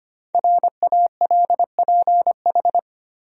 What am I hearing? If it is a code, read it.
Morse code, heard as a keyed tone, about 25 wpm, 700 Hz. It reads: RALPH
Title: USA Common Names - 25wpm